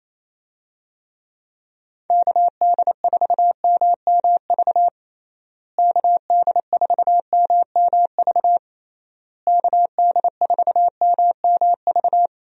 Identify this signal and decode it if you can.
Morse code, heard as a keyed tone, about 28 wpm, 700 Hz. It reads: KD4MMV KD4MMV KD4MMV
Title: USA Call Signs - 6 Characters - Repeated 3x - 28wpm